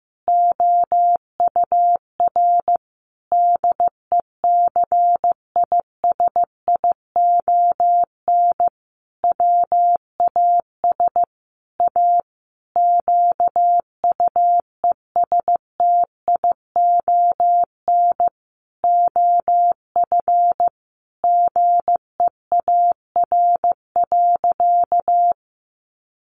Morse code, heard as a keyed tone, about 15 wpm, 700 Hz. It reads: OUR DECISION WAS A QUESTION OF GEAR.
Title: Sentences from Top 1400 Words 15wpm